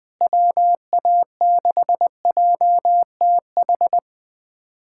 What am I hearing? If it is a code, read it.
Morse code, heard as a keyed tone, about 20 wpm, 700 Hz. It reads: WA6JTH
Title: USA Callsign DE Callsign 20wpm